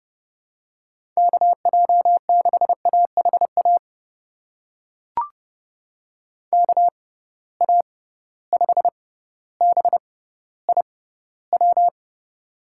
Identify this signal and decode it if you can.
Morse code, heard as a keyed tone, about 31 wpm, spaced as for 16 wpm, 700 Hz. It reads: KJ6AHA KA5BIW
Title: USA Call Signs - 6 Characters - 6x Character Spacing - 30wpm